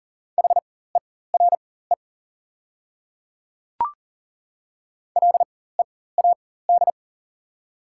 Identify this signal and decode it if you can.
Morse code, heard as a keyed tone, about 40 wpm, spaced as for 19 wpm, 700 Hz. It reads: HERE LEAD
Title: 4x Character Spacing - Top 500 Words - 40wpm